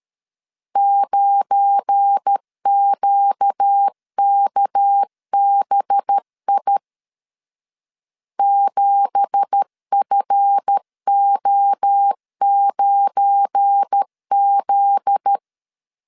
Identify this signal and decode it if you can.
Morse code, heard as a keyed tone, about 13 wpm, 800 Hz. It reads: 9QKBI 7FO9Z
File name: CW / DET / 1 9 1500C S B 12wpm 3819